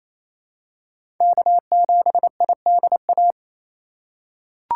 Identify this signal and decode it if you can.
Morse code, heard as a keyed tone, about 28 wpm, 700 Hz. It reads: K7IDA
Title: USA Call Signs - 5 Characters - 4x Character Spacing - 28wpm